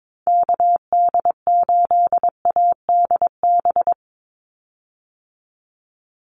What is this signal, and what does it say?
Morse code, heard as a keyed tone, about 22 wpm, 700 Hz. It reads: KD8ADB
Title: USA Call Signs 22wpm